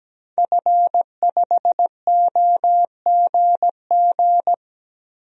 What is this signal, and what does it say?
Morse code, heard as a keyed tone, about 17 wpm, 700 Hz. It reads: F5OGG
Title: DXCC-100 Call Signs - 5 Characters - 17wpm